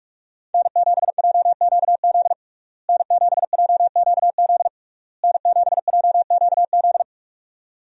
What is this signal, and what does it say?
Morse code, heard as a keyed tone, about 46 wpm, 700 Hz. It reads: N7JQZ N7JQZ N7JQZ
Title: USA Call Signs - 5 Characters - Repeated 3x - 45wpm